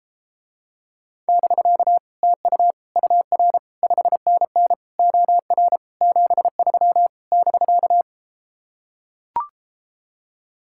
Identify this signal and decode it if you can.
Morse code, heard as a keyed tone, about 33 wpm, 700 Hz. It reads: <BK> TU UR 5NN OR 73 <BK>